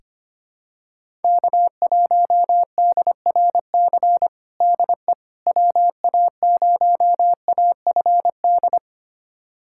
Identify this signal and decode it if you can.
Morse code, heard as a keyed tone, about 25 wpm, 700 Hz. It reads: K1DRC DE WA0AFD
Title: USA Callsign DE Callsign 25wpm